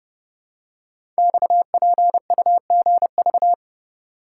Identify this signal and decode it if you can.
Morse code, heard as a keyed tone, about 30 wpm, 700 Hz. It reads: XPUGV
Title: Code Groups - 5 Letters - 30wpm